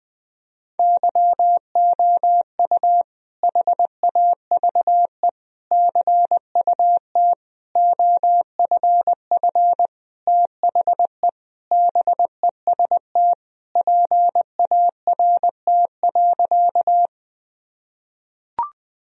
Morse code, heard as a keyed tone, about 21 wpm, 700 Hz. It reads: YOU HAVE CUT OFF THE BEST PART.
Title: Sentences from Top 1000 Words - Review Part 1 - 20wpm